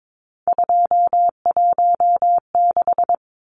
Morse code, heard as a keyed tone, about 22 wpm, 700 Hz. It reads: 216
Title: Numbers - 3-Digit 22wpm